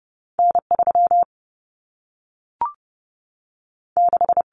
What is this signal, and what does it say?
Morse code, heard as a keyed tone, about 30 wpm, spaced as for 3 wpm, 700 Hz. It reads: N 36
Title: Common Cut Numbers - 2-Digit 30wpm